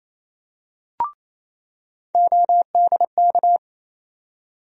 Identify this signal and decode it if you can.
Morse code, heard as a keyed tone, about 28 wpm, 700 Hz. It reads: ODK
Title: USA Call Sign Postfixes - 3 Letters 28wpm